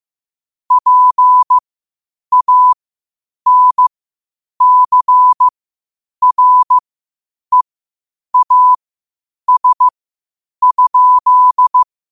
Morse code, heard as a keyed tone, about 15 wpm, spaced as for 8 wpm, 1000 Hz. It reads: PANCREAS?
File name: qotdfiles / 8wpm